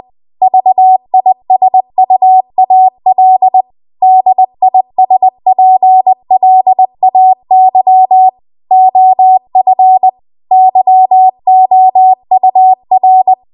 Morse code, heard as a keyed tone, about 20 wpm, 750 Hz. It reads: VISUAL DISPLAY OF YOUR